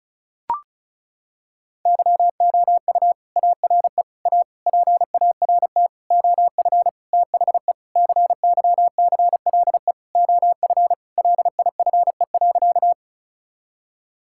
Morse code, heard as a keyed tone, about 36 wpm, 690 Hz. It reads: YOU ARE A PART OF THE CYCLE OF LIFE.
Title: Sentences from Top 1000 Words - Review Part 6 - 35wpm